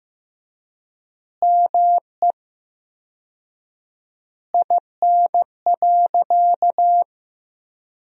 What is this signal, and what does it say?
Morse code, heard as a keyed tone, about 15 wpm, 700 Hz. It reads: ME IN.
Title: 4x Word-Spacing - Sentences from Top 300 Words - 15wpm